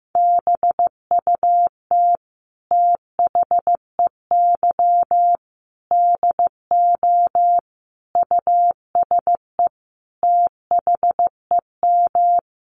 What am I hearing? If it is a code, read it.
Morse code, heard as a keyed tone, about 15 wpm, 700 Hz. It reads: BUT THEY DO USE THEM
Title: Sentences from Top 1300 Words 15wpm